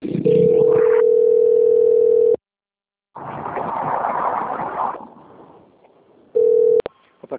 Ringback tone (North America), 440 + 480 Hz.
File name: out- - - - - .39741